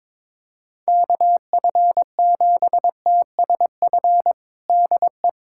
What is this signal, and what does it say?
Morse code, heard as a keyed tone, about 22 wpm, 700 Hz. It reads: KF7TSF DE
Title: USA Callsign DE Callsign - 6 Characters - 22wpm